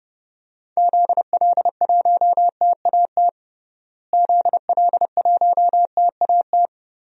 Morse code, heard as a keyed tone, about 30 wpm, 700 Hz. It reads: ZL1TAT ZL1TAT
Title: DXCC-100 Call Signs - 6 Characters - Repeated 3x - 30wpm